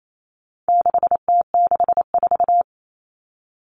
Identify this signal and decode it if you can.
Morse code, heard as a keyed tone, about 28 wpm, 700 Hz. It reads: 6T64
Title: All Cut Numbers - 4-Digit 28wpm